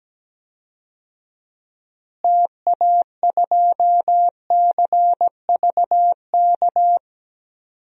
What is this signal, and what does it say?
Morse code, heard as a keyed tone, about 17 wpm, 700 Hz. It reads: TA2CVK
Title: DXCC-100 Call Signs - 6 Characters - Repeated 3x - 17wpm